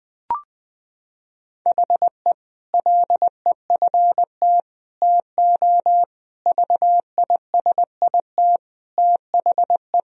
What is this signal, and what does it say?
Morse code, heard as a keyed tone, about 21 wpm, 700 Hz. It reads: HE LEFT TO VISIT THE
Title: Sentences from Top 1000 Words - Review Part 2 - 20wpm